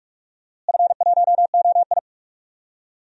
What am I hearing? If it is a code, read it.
Morse code, heard as a keyed tone, about 42 wpm, 700 Hz. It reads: F1OI
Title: DXCC-100 Call Signs - 4 Characters - Repeated 2x - 45wpm